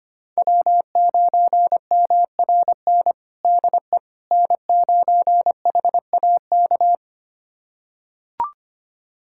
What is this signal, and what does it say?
Morse code, heard as a keyed tone, about 25 wpm, 700 Hz. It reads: W9MRN DE N9HAK
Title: USA Callsign DE Callsign - 5 Characters - 25wpm